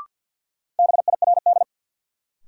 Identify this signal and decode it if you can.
Morse code, heard as a keyed tone, about 52 wpm, 700 Hz. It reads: BIRD
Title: Sentences from Top 400 Words - New Word Drill - 50wpm